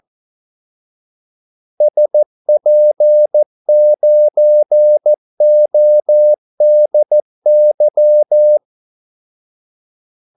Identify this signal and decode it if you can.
Morse code, heard as a keyed tone, about 14 wpm, 600 Hz. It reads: SP9ODY
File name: Callsigns-14wpm